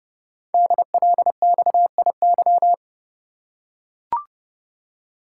Morse code, heard as a keyed tone, about 30 wpm, 700 Hz. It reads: DLXIY
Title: Code Groups - 5 Letters - 30wpm